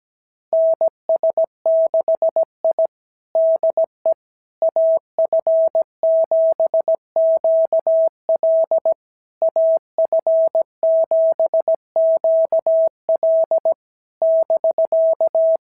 Morse code, heard as a keyed tone, about 17 wpm, 650 Hz. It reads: NS6I DE AF7QL AF7QL <BK>
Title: SOTA QSO Practice 17wpm